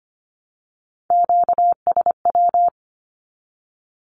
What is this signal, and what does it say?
Morse code, heard as a keyed tone, about 25 wpm, 700 Hz. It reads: QSW
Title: Q-Codes 25wpm